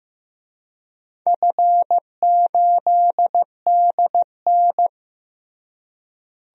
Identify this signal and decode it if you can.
Morse code, heard as a keyed tone, about 15 wpm, 700 Hz. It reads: F8DN
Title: DXCC-100 Call Signs - 4 Characters - Repeated 2x - 15wpm